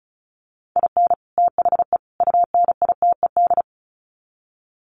Morse code, heard as a keyed tone, about 34 wpm, 700 Hz. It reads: IN THE UNITED
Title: Sets of 3 Words 35wpm